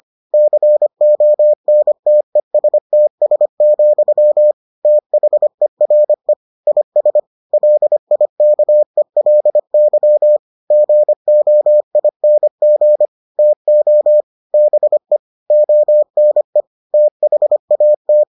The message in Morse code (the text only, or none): CONTESTS, THERE IS LIKELY GOING TO BE ONE THAT